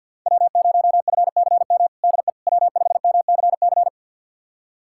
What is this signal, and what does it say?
Morse code, heard as a keyed tone, about 52 wpm, 700 Hz. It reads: W0FCM DE W5MCC
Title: USA Callsign DE Callsign - 5 Characters - 50wpm